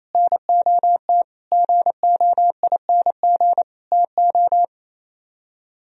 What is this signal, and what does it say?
Morse code, heard as a keyed tone, about 33 wpm, 700 Hz. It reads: NOT GOING TO
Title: Speed-Racing - Sets of 5 Words 28wpm